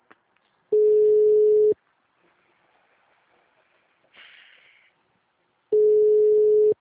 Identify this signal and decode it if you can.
Ringback tone (Europe, 425 Hz).